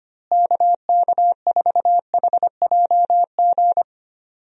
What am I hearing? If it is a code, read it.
Morse code, heard as a keyed tone, about 25 wpm, 700 Hz. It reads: KK4HJG
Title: USA Call Signs - 6 Characters - Repeated 3x - 25wpm